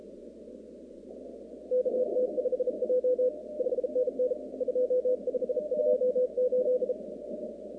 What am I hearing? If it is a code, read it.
Morse code, heard as a keyed tone, about 37 wpm, 500 Hz. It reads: KT5J 5NN 2528
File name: contests / 2019-cw-wpx-kt5j-k5pi- -2330 0-38